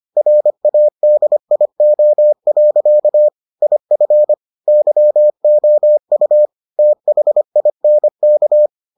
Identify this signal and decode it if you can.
Morse code, heard as a keyed tone, about 25 wpm, 600 Hz. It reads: RADIO. IF YOU THINK